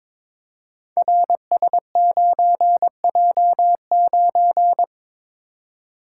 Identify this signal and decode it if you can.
Morse code, heard as a keyed tone, about 22 wpm, 700 Hz. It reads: RS9J9
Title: Code Groups - 5 Letter:Numbers - 22wpm